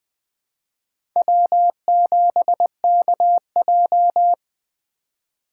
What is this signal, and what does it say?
Morse code, heard as a keyed tone, about 21 wpm, 700 Hz. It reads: W7KJ
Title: USA Call Signs - 4 Characters - 4x Character Spacing - 20wpm